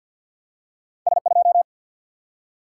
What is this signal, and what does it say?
Morse code, heard as a keyed tone, about 51 wpm, 700 Hz. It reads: S 2
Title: Single-Letter Single-Number 50wpm